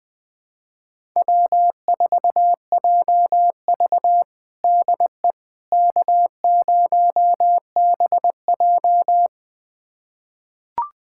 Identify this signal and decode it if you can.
Morse code, heard as a keyed tone, about 21 wpm, 700 Hz. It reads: W4JV DE K0BJ
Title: USA Callsign DE Callsign - 4 Characters - 20wpm